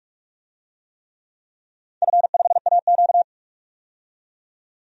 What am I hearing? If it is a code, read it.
Morse code, heard as a keyed tone, about 42 wpm, 700 Hz. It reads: F5AQ
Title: DXCC-100 Call Signs - 4 Characters - 45wpm